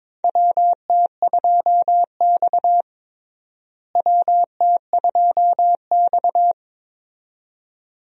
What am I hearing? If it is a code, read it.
Morse code, heard as a keyed tone, about 22 wpm, 700 Hz. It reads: WT2X WT2X
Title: USA Call Signs - 4 Characters - Repeated 2x - 22wpm